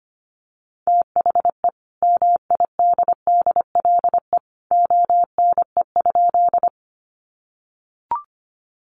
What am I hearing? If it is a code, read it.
Morse code, heard as a keyed tone, about 25 wpm, 700 Hz. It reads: THE MIDDLE ONE?
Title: Sentences from Top 900 Words 25wpm